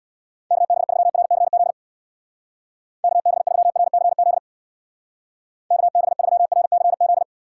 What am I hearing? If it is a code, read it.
Morse code, heard as a keyed tone, about 60 wpm, 700 Hz. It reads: KB3RCZ KB3RCZ KB3RCZ
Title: Speed Racing - USA 2x3 Call Signs 50wpm